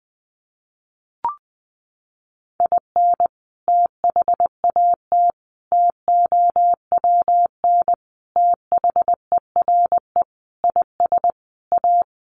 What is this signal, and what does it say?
Morse code, heard as a keyed tone, about 20 wpm, 700 Hz. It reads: IN THAT TOWN THERE IS A
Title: Sentences from Top 500 Words 20wpm